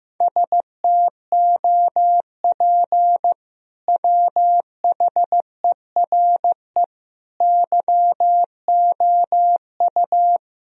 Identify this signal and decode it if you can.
Morse code, heard as a keyed tone, about 15 wpm, 700 Hz. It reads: STOP WHERE YOU
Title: Sentences from Top 1000 Words - Review Part 1 - 15wpm